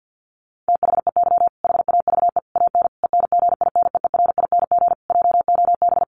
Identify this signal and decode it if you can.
Morse code, heard as a keyed tone, about 50 wpm, 700 Hz. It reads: THEY HAVE AN ENGINEERING JOB
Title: Sentences from Top 1500 Words 50wpm